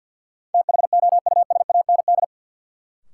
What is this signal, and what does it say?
Morse code, heard as a keyed tone, about 53 wpm, 700 Hz. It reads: THOUSAND
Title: Sentences from Top 600 Words - New Word Drill - 50wpm